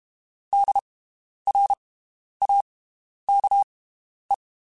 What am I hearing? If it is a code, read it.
Morse code, heard as a keyed tone, about 32 wpm, spaced as for 11 wpm, 800 Hz. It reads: DRAKE
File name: (spaced) / QSOs Spaced-mp3s / WD0FBT 1 @30 sp